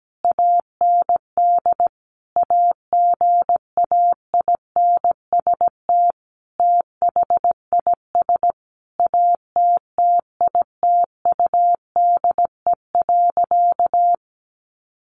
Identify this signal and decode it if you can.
Morse code, heard as a keyed tone, about 17 wpm, 700 Hz. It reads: AND AGAINST THIS ATTITUDE.
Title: Sentences from Top 1300 Words 17wpm